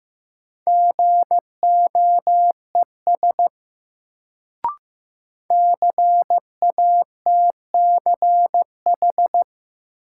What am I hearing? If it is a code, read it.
Morse code, heard as a keyed tone, about 15 wpm, 700 Hz. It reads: GOES CATCH
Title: Sentences from Top 1500 Words - New Word Drill Alt - 15wpm